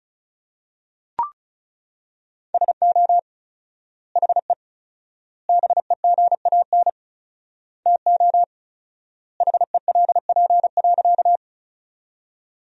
Morse code, heard as a keyed tone, about 36 wpm, 700 Hz. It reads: SO HE BEGAN TO HELP.
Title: 4x Word-Spacing - Sentences from Top 300 Words - 35wpm